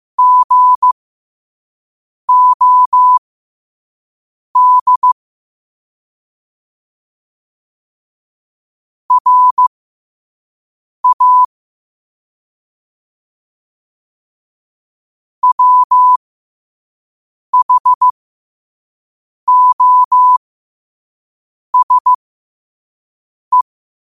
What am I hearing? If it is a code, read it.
Morse code, heard as a keyed tone, about 15 wpm, spaced as for 5 wpm, 1000 Hz. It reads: GOD RA WHOSE
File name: qotdfiles / 5wpm